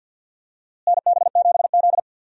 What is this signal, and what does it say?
Morse code, heard as a keyed tone, about 52 wpm, 700 Hz. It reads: NB7Z
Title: USA Call Signs - 4 Characters - Repeated 2x - 50wpm